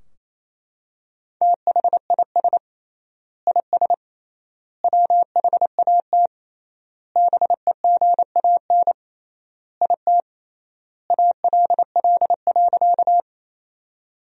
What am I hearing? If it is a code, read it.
Morse code, heard as a keyed tone, about 27 wpm, 700 Hz. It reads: THIS IS WHAT BEGAN IT ALL.
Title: 3x Word-Spacing - Sentences from Top 300 Words - 28wpm